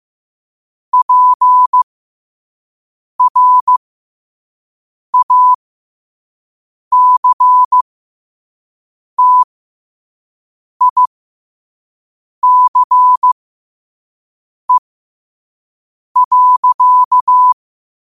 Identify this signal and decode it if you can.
Morse code, heard as a keyed tone, about 15 wpm, spaced as for 5 wpm, 1000 Hz. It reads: PRACTICE.
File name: qotdfiles / 5wpm